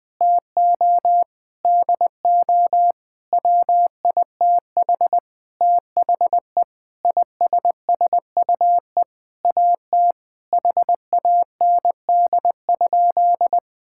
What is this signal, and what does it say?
Morse code, heard as a keyed tone, about 20 wpm, 700 Hz. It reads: TO DO WITH THE ISSUE AT HAND?
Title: Sentences with 9-Word Phrases - Spoken Beginning - 20wpm